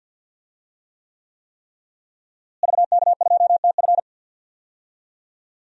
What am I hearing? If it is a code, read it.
Morse code, heard as a keyed tone, about 47 wpm, 700 Hz. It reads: VK2TF
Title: DXCC-100 Call Signs - 5 Characters - Repeated 3x - 50wpm